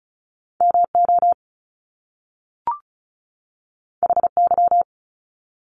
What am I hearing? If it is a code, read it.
Morse code, heard as a keyed tone, about 36 wpm, 700 Hz. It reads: MO HY
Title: Top Two Letter Combinations in Words 35wpm